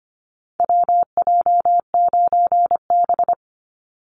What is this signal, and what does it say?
Morse code, heard as a keyed tone, about 25 wpm, 700 Hz. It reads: WJ9B
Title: Call Signs - 25wpm